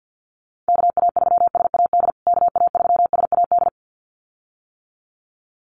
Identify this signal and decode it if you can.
Morse code, heard as a keyed tone, about 49 wpm, 700 Hz. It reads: KA3SAD KA3SAD
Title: USA Call Signs Repeated 2x 50wpm